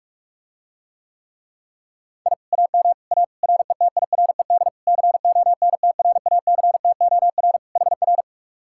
Morse code, heard as a keyed tone, about 44 wpm, 700 Hz. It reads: I AM A RETIRED CONTRACTOR HR